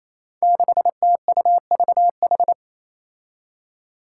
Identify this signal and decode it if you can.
Morse code, heard as a keyed tone, about 27 wpm, 700 Hz. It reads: 6TUVH
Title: Five Challenging Letter-Number Combinations 28wpm